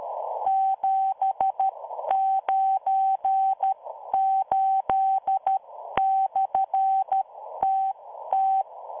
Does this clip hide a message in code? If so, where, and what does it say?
Yes: Morse code from 0.5 to 8.6 s, heard as a keyed tone, about 13 wpm, spaced as for 11 wpm, 770 Hz. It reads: 798/TT